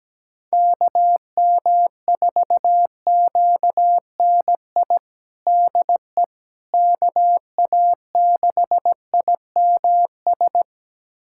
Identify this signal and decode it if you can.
Morse code, heard as a keyed tone, about 17 wpm, 700 Hz. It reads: KM4QNI DE KA6IMS
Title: USA Callsign DE Callsign - 6 Characters - 17wpm